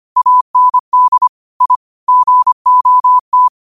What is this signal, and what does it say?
Morse code, heard as a keyed tone, about 25 wpm, 1000 Hz. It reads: AND I GOT